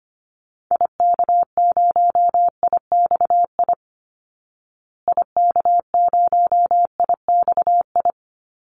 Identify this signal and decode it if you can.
Morse code, heard as a keyed tone, about 25 wpm, 700 Hz. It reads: IK0IXI IK0IXI
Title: Call Signs Repeated 3x - 25wpm